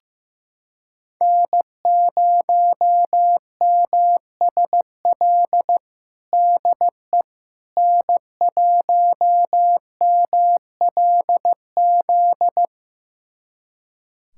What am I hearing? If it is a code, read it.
Morse code, heard as a keyed tone, about 15 wpm, 700 Hz. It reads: N0MSL DE N1MLZ
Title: USA Callsign DE Callsign - 5 Characters - 15wpm